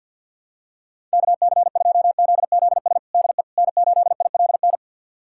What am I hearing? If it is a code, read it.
Morse code, heard as a keyed tone, about 45 wpm, 700 Hz. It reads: KC2ZZS DE N8ILN
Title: USA Callsign DE Callsign 50wpm